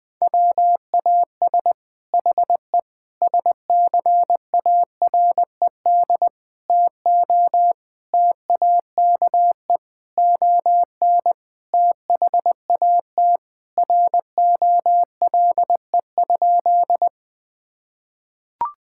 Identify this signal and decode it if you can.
Morse code, heard as a keyed tone, about 20 wpm, 700 Hz. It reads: WAS HE SCARED TO TAKE ON THAT ROLE?
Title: Sentences from Top 1000 Words - Review Part 6 - 20wpm